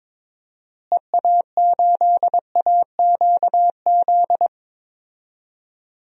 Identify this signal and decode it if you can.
Morse code, heard as a keyed tone, about 22 wpm, 700 Hz. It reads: EA8AQZ
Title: DXCC-100 Call Signs - 6 Characters - Repeated 2x - 22wpm